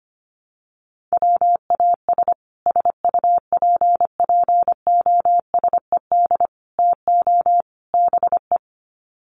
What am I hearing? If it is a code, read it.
Morse code, heard as a keyed tone, about 25 wpm, 700 Hz. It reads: WAS SUPPOSED TO BE
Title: Sets of 4 Words 25wpm